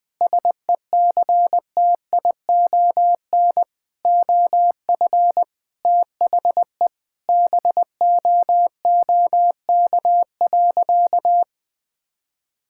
Morse code, heard as a keyed tone, about 20 wpm, 700 Hz. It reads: SECTION OF THE BOOK.
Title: Sentences from Top 1000 Words - Review Part 5 - 20wpm